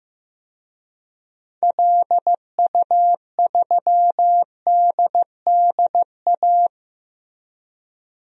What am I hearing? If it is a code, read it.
Morse code, heard as a keyed tone, about 15 wpm, 700 Hz. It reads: LU3DDA
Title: DXCC-100 Call Signs - 6 Characters - Repeated 2x - 15wpm